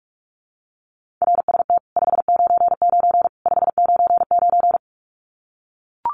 Morse code, heard as a keyed tone, about 44 wpm, 700 Hz. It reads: RST 599 599
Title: Common QSO Phrases 45wpm